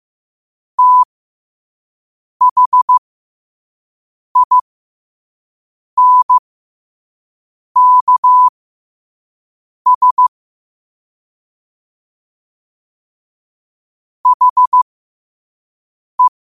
Morse code, heard as a keyed tone, about 15 wpm, spaced as for 5 wpm, 1000 Hz. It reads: THINKS HE
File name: qotdfiles / 5wpm